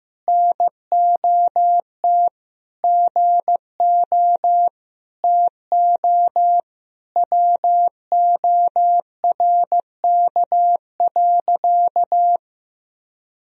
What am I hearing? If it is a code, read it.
Morse code, heard as a keyed tone, about 15 wpm, 700 Hz. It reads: NOT GO TO WORK.
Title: Sentences from Top 1000 Words - Review Part 2 - 15wpm